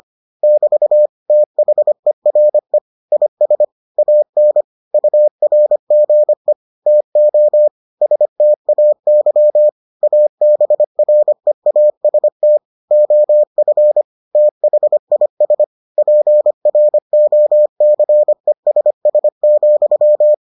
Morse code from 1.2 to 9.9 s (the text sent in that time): THERE IS AN URGE TO STAY